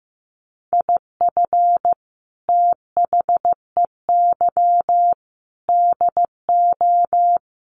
Morse code, heard as a keyed tone, about 15 wpm, 700 Hz. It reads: IF THEY DO
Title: Sets of 4 Words 15wpm